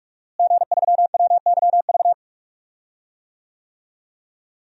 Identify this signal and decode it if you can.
Morse code, heard as a keyed tone, about 46 wpm, 700 Hz. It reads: G3WYV
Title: DXCC-100 Call Signs - 5 Characters - Repeated 3x - 45wpm